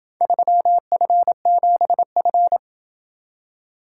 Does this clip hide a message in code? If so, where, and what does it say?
Yes: Morse code from 0.2 to 2.6 s, heard as a keyed tone, about 27 wpm, 700 Hz. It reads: 3F7F